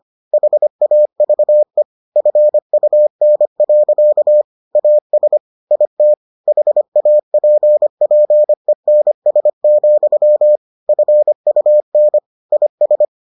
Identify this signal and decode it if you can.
Morse code, heard as a keyed tone, about 25 wpm, 600 Hz. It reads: HAVE FUN. AS IT HAPPENS, FUN IS